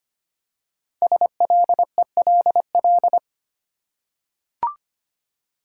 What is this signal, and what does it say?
Morse code, heard as a keyed tone, about 25 wpm, 700 Hz. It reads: SLELL
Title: Code Groups - 5 Letters - 25wpm